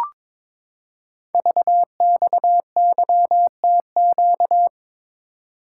Morse code, heard as a keyed tone, about 22 wpm, 700 Hz. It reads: VXYTQ
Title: Code Groups - 5 Letter:Numbers - 22wpm